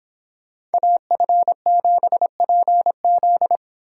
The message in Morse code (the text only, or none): AF7PZ